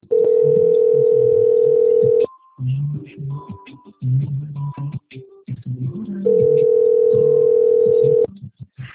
Ringback tone (North America), 440 + 480 Hz.
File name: out- - - - - .2229